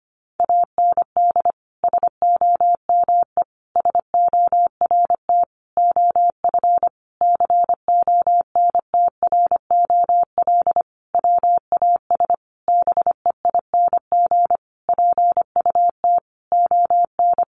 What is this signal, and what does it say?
Morse code, heard as a keyed tone, about 25 wpm, 700 Hz. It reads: AND SOME SORT OF CONTROL WAS BEING PUT ON
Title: Sentences from Top 1300 Words 25wpm